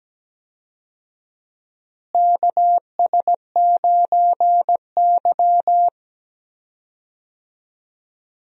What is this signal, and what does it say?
Morse code, heard as a keyed tone, about 17 wpm, 700 Hz. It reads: KS9Y
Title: USA Call Signs - 4 Characters - Repeated 3x - 17wpm